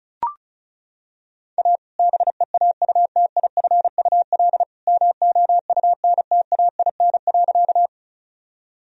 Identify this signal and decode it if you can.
Morse code, heard as a keyed tone, about 34 wpm, 700 Hz. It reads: A BEAUTIFUL MOUNTAIN.
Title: Sentences from Top 1000 Words - Review Part 1 - 35wpm